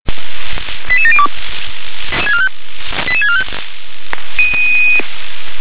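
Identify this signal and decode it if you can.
Five-tone selective call (ZVEI), heard as quick runs of five tones, tones of about 70 ms. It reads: 88702 88645
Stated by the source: ZVEI five-tone selective call received with an SDR.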